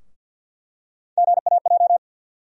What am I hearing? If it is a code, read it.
Morse code, heard as a keyed tone, about 51 wpm, 700 Hz. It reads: GAJ
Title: USA Call Sign Postfixes - 3 Letters 50wpm